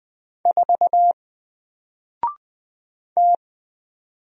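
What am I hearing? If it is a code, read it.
Morse code, heard as a keyed tone, about 20 wpm, 700 Hz. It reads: 4 T
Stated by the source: Single letter and number Morse drill at 20 wpm, from N4LCD.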